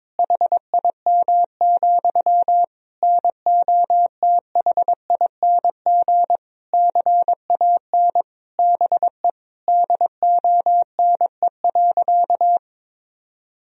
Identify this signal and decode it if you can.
Morse code, heard as a keyed tone, about 21 wpm, 700 Hz. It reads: HIM, NOTHING CAN BE DONE.
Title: Sentences from Top 1000 Words - Review Part 1 - 22wpm